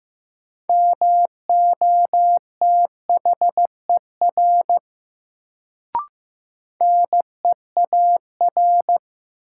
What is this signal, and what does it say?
Morse code, heard as a keyed tone, about 16 wpm, 700 Hz. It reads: MOTHER NEAR
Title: Sentences from Top 200 Words - New Word Drill - 15wpm